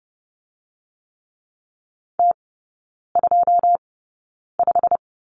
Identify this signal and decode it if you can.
Morse code, heard as a keyed tone, about 30 wpm, spaced as for 9 wpm, 700 Hz. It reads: T25
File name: Lesson 18e - Letter m - Three Characters 30wpm